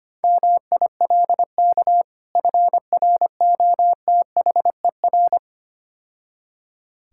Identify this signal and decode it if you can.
Morse code, heard as a keyed tone, about 26 wpm, 700 Hz. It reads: MILK FROTHER
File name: practice / 25wpm / Difficult Words - Speed-Racing with Hints - 25wpm